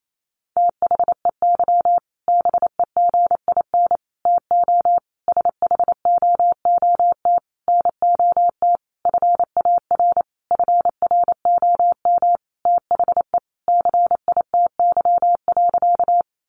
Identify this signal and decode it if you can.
Morse code, heard as a keyed tone, about 28 wpm, 700 Hz. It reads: THEY BEGIN TO SHOOT NOT FAR FROM THE CITY.
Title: Sentences from Top 1200 Words 28wpm